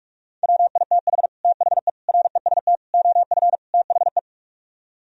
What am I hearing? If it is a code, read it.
Morse code, heard as a keyed tone, about 48 wpm, 700 Hz. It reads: WITH THE REST OF THE